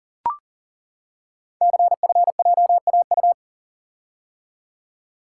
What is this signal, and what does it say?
Morse code, heard as a keyed tone, about 40 wpm, 690 Hz. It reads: CFJAU
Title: Code Groups - 5 Letters - 40wpm